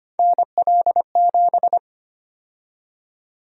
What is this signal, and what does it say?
Morse code, heard as a keyed tone, about 25 wpm, 700 Hz. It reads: NL7
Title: USA Call Sign Prefixes - Repeated 2x - 25wpm